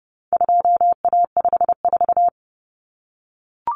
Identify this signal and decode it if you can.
Morse code, heard as a keyed tone, about 30 wpm, 700 Hz. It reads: 2A54
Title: Common Cut Numbers - 4-Digit 30wpm